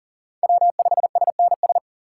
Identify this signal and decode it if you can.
Morse code, heard as a keyed tone, about 40 wpm, 700 Hz. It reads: W5SNS
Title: USA Call Signs - 5 Characters - 2x Character Spacing - 40wpm